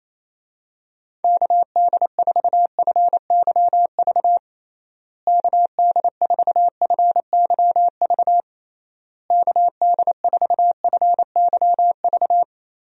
Morse code, heard as a keyed tone, about 27 wpm, 700 Hz. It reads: KD4FYV KD4FYV KD4FYV
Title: USA Call Signs - 6 Characters - Repeated 3x - 28wpm